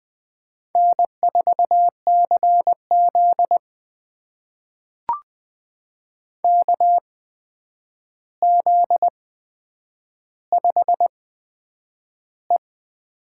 Morse code, heard as a keyed tone, about 20 wpm, spaced as for 9 wpm, 700 Hz. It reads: N4CZ KZ5E
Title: USA Call Signs - 8x Character Spacing - 20wpm